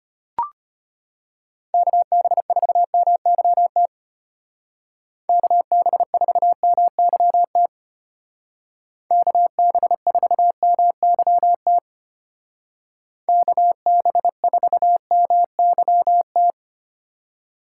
Morse code, heard as a keyed tone, about 30 wpm, 700 Hz. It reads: KB4MYT KB4MYT KB4MYT KB4MYT
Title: Speed Racing - USA 2x3 Call Signs 25wpm